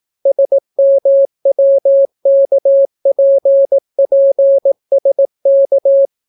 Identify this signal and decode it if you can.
Morse code, heard as a keyed tone, about 18 wpm, 550 Hz. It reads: SMWKPPSK